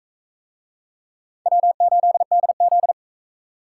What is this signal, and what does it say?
Morse code, heard as a keyed tone, about 37 wpm, 700 Hz. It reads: W8DZ